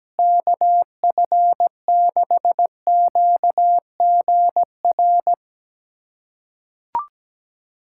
Morse code, heard as a keyed tone, about 17 wpm, 700 Hz. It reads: KF6QGR
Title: USA Callsign DE Callsign 17wpm